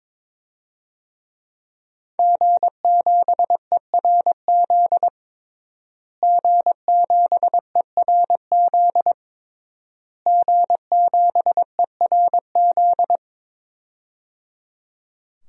Morse code, heard as a keyed tone, about 22 wpm, 700 Hz. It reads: G7ERZ G7ERZ G7ERZ
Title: DXCC-100 Call Signs - 5 Characters - Repeated 3x - 22wpm